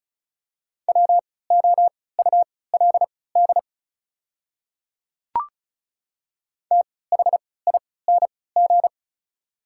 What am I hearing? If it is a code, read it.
Morse code, heard as a keyed tone, about 36 wpm, spaced as for 20 wpm, 690 Hz. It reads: WOULD THING
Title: 3x Character Spacing - Top 100 Words - 35wpm